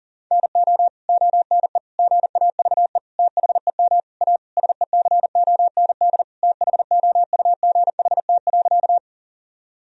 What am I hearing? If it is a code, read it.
Morse code, heard as a keyed tone, about 40 wpm, 700 Hz. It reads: NO ONE GAVE THEM A SECOND THOUGHT.
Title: Sentences from Top 1000 Words - Review Part 1 - 40wpm